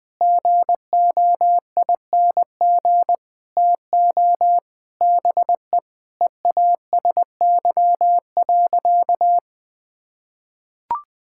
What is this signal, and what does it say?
Morse code, heard as a keyed tone, about 21 wpm, 700 Hz. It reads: GOING TO BE EASY.
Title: Sentences with 5-Word Phrases - 20wpm